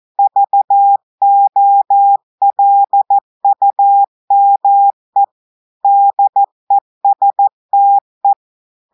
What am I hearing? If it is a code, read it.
Morse code, heard as a keyed tone, about 14 wpm, 800 Hz. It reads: VOLUME DESTE
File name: audios / portugais / portugais texte en clair 1 vitesse 14 mots minutes